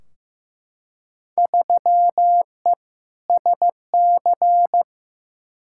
Morse code, heard as a keyed tone, about 15 wpm, 700 Hz. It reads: 3E SC
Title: Field Day Exchanges Encore - 15wpm